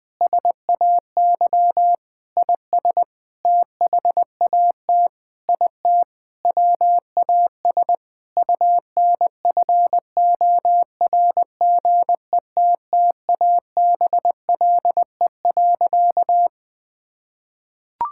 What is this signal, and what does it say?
Morse code, heard as a keyed tone, about 20 wpm, 700 Hz. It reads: SAY IS THAT IT WAS UNFORGETTABLE.
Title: Sentences with 6-Word Phrases - Spoken Beginning - 20wpm